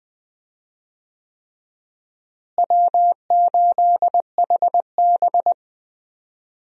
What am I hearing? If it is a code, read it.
Morse code, heard as a keyed tone, about 21 wpm, 700 Hz. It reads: W8HB
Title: USA Call Signs - 4 Characters - Repeated 3x - 20wpm